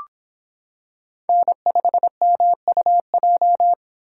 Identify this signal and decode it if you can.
Morse code, heard as a keyed tone, about 26 wpm, 700 Hz. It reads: N5MUJ